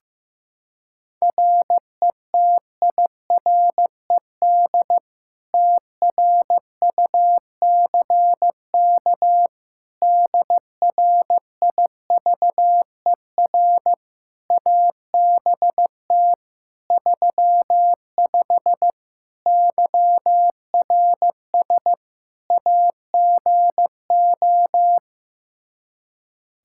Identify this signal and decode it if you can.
Morse code, heard as a keyed tone, about 15 wpm, 700 Hz. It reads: RETIRED TRUCK DRIVER ABT 35 YRS AGO
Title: Practice QSO - Exchange 3 - More Info - 15wpm